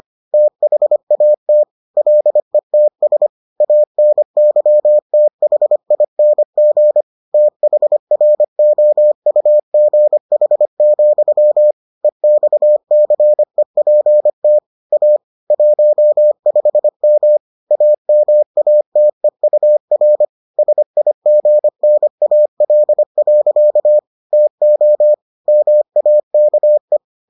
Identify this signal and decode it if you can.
Morse code, heard as a keyed tone, about 25 wpm, 600 Hz. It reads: THAT LETS ANYTHING THROUGH, EXCEPT A 15M AMATEUR SIGNAL. TO MAKE